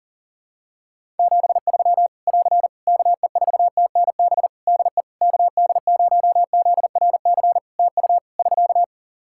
Morse code, heard as a keyed tone, about 40 wpm, 700 Hz. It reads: 73 <AR> KE4TNB DE KD0ZRC TU <SK>
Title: K1USN SST QSO Practice 40wpm